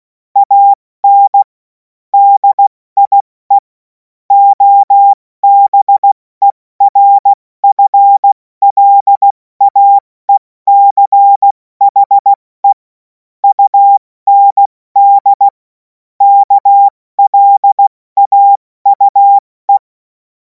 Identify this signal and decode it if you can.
Morse code, heard as a keyed tone, about 16 wpm, spaced as for 14 wpm, 800 Hz. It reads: AN DIE OBERFLAECHE UND KLAUE
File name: Morsekurs / Witze / Witz 03. März 2026 70 F80